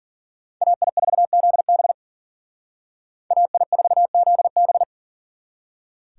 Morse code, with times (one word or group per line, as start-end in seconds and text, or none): 0.6-1.9 AI4ZB
3.3-4.8 AI4ZB